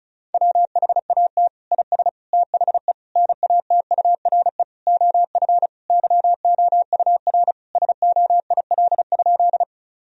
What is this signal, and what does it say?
Morse code, heard as a keyed tone, about 36 wpm, 700 Hz. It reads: WHAT IS THE NATURE OF YOUR SOIL?
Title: Sentences from Top 1000 Words - Review Part 4 - 35wpm